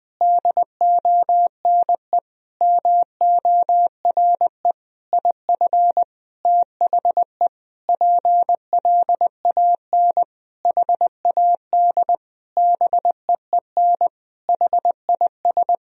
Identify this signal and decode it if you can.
Morse code, heard as a keyed tone, about 20 wpm, 700 Hz. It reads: DONE MORE IF THE PLAN HAD BEEN HIS
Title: Sentences from Top 1000 Words - Review Part 2 - 20wpm